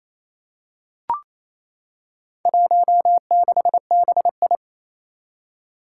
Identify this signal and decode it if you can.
Morse code, heard as a keyed tone, about 28 wpm, 700 Hz. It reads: 16BI
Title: Four Challenging Letter-Number Combinations 28wpm